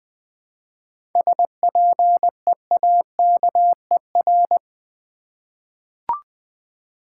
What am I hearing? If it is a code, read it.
Morse code, heard as a keyed tone, about 20 wpm, 700 Hz. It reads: SPEAKER
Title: Seven Letter Words - 20wpm